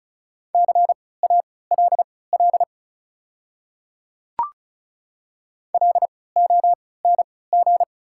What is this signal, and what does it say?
Morse code, heard as a keyed tone, about 34 wpm, spaced as for 20 wpm, 690 Hz. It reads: CALL LONG
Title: 3x Character Spacing - Top 100 Words - 35wpm